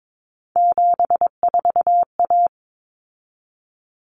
Morse code, heard as a keyed tone, about 22 wpm, 700 Hz. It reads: 74A
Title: Common Cut Numbers - 3-Digit 22wpm